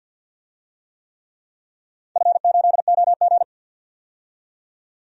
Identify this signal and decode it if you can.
Morse code, heard as a keyed tone, about 48 wpm, 700 Hz. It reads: F8OG
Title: DXCC-100 Call Signs - 4 Characters - 50wpm